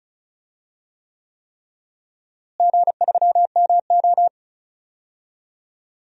Morse code, heard as a keyed tone, about 36 wpm, 700 Hz. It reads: G3MO
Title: DXCC-100 Call Signs - 4 Characters - Repeated 3x - 35wpm